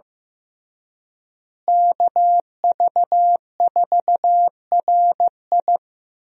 Morse code, heard as a keyed tone, about 15 wpm, 700 Hz. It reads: KV4RI
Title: USA Call Signs - 5 Characters - Repeated 3x - 15wpm